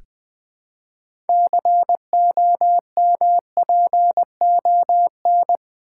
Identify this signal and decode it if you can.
Morse code, heard as a keyed tone, about 20 wpm, 700 Hz. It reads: COMPON
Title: Build A Word - 3 Letters At A Time - 9-Letter Words - 20wpm